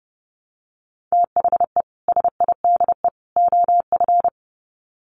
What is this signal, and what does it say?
Morse code, heard as a keyed tone, about 30 wpm, 700 Hz. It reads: THE SIDE OF
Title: Sets of 3 Words 30wpm